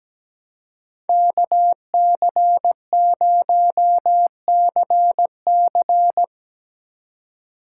Morse code, heard as a keyed tone, about 17 wpm, 700 Hz. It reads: KC0CC
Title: USA Call Signs - 5 Characters - 3x Character Spacing - 17wpm